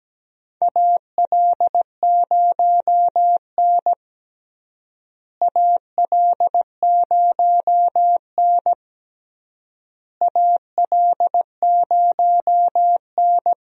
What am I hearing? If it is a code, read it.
Morse code, heard as a keyed tone, about 17 wpm, 700 Hz. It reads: AL0N AL0N AL0N
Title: USA Call Signs - 4 Characters - Repeated 3x - 17wpm